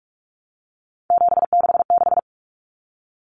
Morse code, heard as a keyed tone, about 44 wpm, 700 Hz. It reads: 766
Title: Numbers - 3-Digit 45wpm